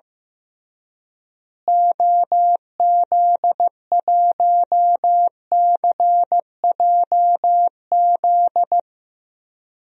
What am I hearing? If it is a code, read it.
Morse code, heard as a keyed tone, about 15 wpm, 700 Hz. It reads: OZ1CJZ
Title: DXCC-100 Call Signs - 6 Characters - Repeated 3x - 15wpm